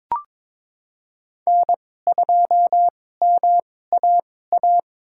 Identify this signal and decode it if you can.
Morse code, heard as a keyed tone, about 22 wpm, spaced as for 16 wpm, 700 Hz. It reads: N2MAA
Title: USA Call Signs - 5 Characters - 2x Character Spacing - 22wpm